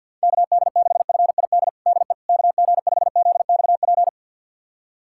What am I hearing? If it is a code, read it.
Morse code, heard as a keyed tone, about 52 wpm, 700 Hz. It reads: KD6FID DE KG5ZXP
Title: USA Callsign DE Callsign - 6 Characters - 50wpm